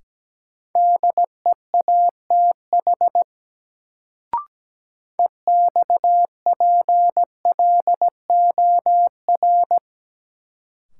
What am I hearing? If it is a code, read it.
Morse code, heard as a keyed tone, about 17 wpm, 700 Hz. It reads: DEATH EXPLOR
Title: Sentences from Top 1100 Words - New Word Drill - 17wpm